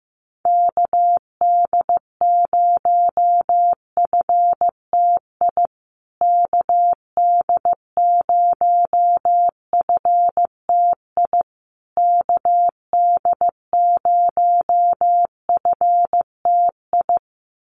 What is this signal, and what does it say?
Morse code, heard as a keyed tone, about 14 wpm, 700 Hz. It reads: KD0FTI KD0FTI KD0FTI
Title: USA Call Signs Repeated 3x 15wpm